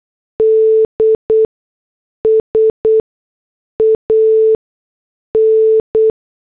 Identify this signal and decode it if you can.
Morse code, heard as a keyed tone, about 8 wpm, spaced as for 6 wpm, 440 Hz. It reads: DS AN